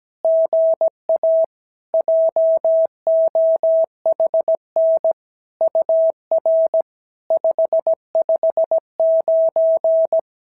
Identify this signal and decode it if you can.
Morse code, heard as a keyed tone, about 17 wpm, 650 Hz. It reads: GA JOHN UR 559
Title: SOTA QSO Practice 17wpm